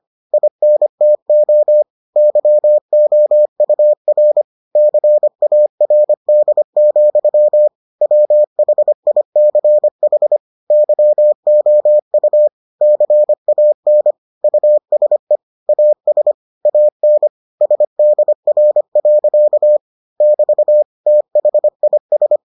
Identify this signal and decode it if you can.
Morse code, heard as a keyed tone, about 25 wpm, 600 Hz. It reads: INTO YOUR CARD, WHICH YOU CAN USE AS AN SDR. <BT> THIS